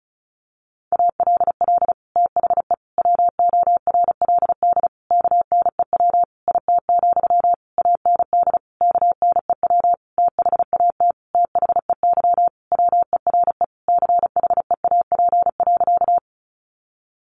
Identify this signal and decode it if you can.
Morse code, heard as a keyed tone, about 34 wpm, 700 Hz. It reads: ALL THE WORLD KNEW IT, AND KNEW THAT THEY WERE CHEAP.
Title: Sentences from Top 1200 Words 35wpm